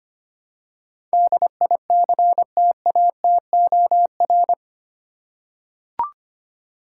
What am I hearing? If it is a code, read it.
Morse code, heard as a keyed tone, about 25 wpm, 700 Hz. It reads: DICTATOR
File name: 8 Letter Words - First 3 Letters Given - 25wpm